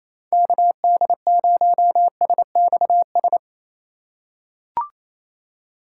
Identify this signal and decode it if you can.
Morse code, heard as a keyed tone, about 28 wpm, 690 Hz. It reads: KD0SXS
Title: USA Call Signs - 6 Characters - 2x Character Spacing - 28wpm